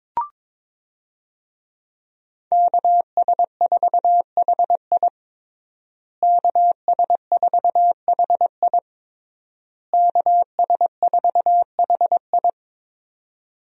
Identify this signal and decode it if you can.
Morse code, heard as a keyed tone, about 22 wpm, 700 Hz. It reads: KS4HI KS4HI KS4HI
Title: USA Call Signs - 5 Characters - Repeated 3x - 22wpm